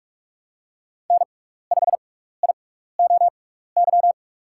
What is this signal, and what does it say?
Morse code, heard as a keyed tone, about 47 wpm, spaced as for 16 wpm, 700 Hz. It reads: N5IOY
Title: USA Call Signs - 5 Characters - 6x Character Spacing - 45wpm